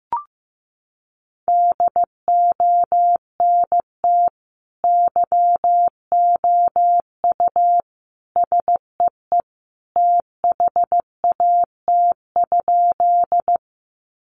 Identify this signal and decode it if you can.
Morse code, heard as a keyed tone, about 15 wpm, 700 Hz. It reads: DONT YOU SEE THAT?
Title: Sentences from Top 300 Words 15wpm